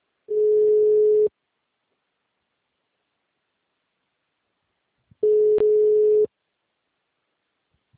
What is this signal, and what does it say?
Ringback tone (Europe, 425 Hz).